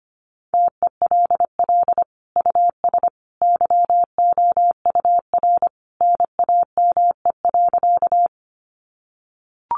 Morse code, heard as a keyed tone, about 25 wpm, 700 Hz. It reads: TELL US YOUR NAME.
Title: Sentences from Top 200 Words 25wpm